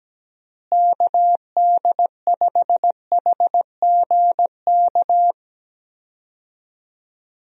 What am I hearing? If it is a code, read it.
Morse code, heard as a keyed tone, about 17 wpm, 700 Hz. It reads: KD5HGK
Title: USA Call Signs - 6 Characters - Repeated 3x - 17wpm